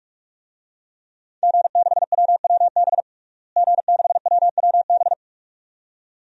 Morse code, heard as a keyed tone, about 46 wpm, 700 Hz. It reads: G6WWB G6WWB
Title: DXCC-100 Call Signs - 5 Characters - Repeated 2x - 45wpm